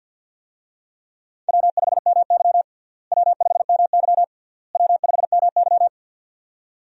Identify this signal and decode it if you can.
Morse code, heard as a keyed tone, about 51 wpm, 700 Hz. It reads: W5MY W5MY W5MY
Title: USA Call Signs - 4 Characters - Repeated 3x - 50wpm